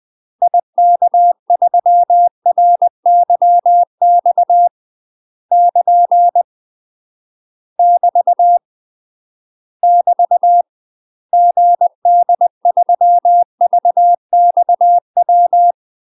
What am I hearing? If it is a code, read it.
Morse code, heard as a keyed tone, about 20 wpm, 700 Hz. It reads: IK3RYX <KN> <BT> <BT> GD3VXW
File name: UK MASTER QSOs TRAINING / 20WPM / UK MASTER QSOs 0031